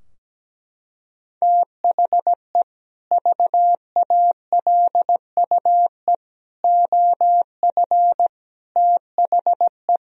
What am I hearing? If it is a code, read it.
Morse code, heard as a keyed tone, about 17 wpm, 700 Hz. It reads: THE VALUE OF THE
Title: Sentences from Top 1000 Words - Review Part 3 - 17wpm